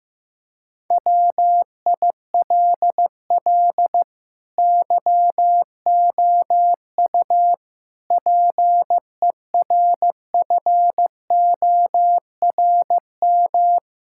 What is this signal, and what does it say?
Morse code, heard as a keyed tone, about 15 wpm, 700 Hz. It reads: WILL YOU PERFORM
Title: Sentences from Top 1000 Words - Review Part 5 - 15wpm